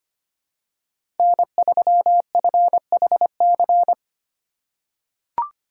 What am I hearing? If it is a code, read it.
Morse code, heard as a keyed tone, about 25 wpm, 700 Hz. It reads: N3FHC
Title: USA Call Signs - 5 Characters - 4x Character Spacing - 25wpm